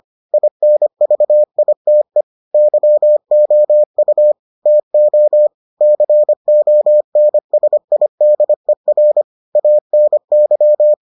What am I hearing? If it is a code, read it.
Morse code, heard as a keyed tone, about 25 wpm, 600 Hz. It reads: INVITE YOU TO CONSIDER ANY